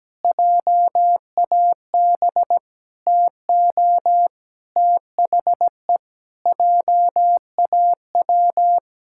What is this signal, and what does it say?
Morse code, heard as a keyed tone, about 17 wpm, 700 Hz. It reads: JAB TO THE JAW
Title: Pangram Challenge v2 - 17wpm